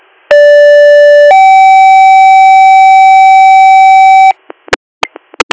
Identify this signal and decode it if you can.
Two-tone page, heard as a one-second tone, then a three-second tone, about 580 then 770 Hz. It is heard from 0.3 to 4.3 s.